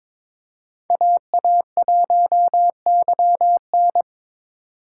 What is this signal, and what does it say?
Morse code, heard as a keyed tone, about 21 wpm, 700 Hz. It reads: AA1YN
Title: USA Call Signs - 5 Characters - 4x Character Spacing - 22wpm